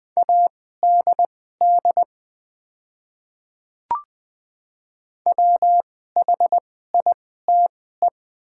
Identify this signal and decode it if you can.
Morse code, heard as a keyed tone, about 20 wpm, spaced as for 14 wpm, 700 Hz. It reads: ADD WHITE
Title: 2x Character Spacing - Top 500 Words - 20wpm